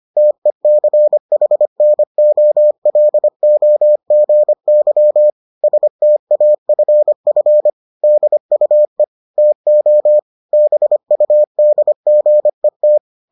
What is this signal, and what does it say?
Morse code, heard as a keyed tone, about 25 wpm, 600 Hz. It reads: TECHNOLOGY STAFF DUE TO BUDGET